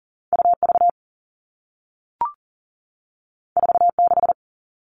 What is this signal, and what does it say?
Morse code, heard as a keyed tone, about 40 wpm, 700 Hz. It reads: UV 46
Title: All Cut Numbers - 2-Digit 40wpm